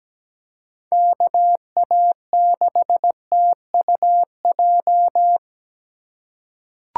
Morse code, heard as a keyed tone, about 17 wpm, 700 Hz. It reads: KA6TUJ
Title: USA Call Signs - 6 Characters - 3x Character Spacing - 17wpm